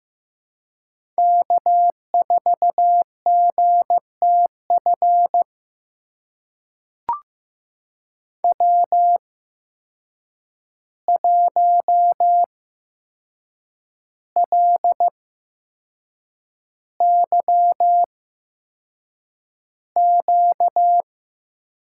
Morse code, heard as a keyed tone, about 15 wpm, spaced as for 6 wpm, 700 Hz. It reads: K4GTF W1LYQ
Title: USA Call Signs - 5 Characters - 8x Character Spacing - 15wpm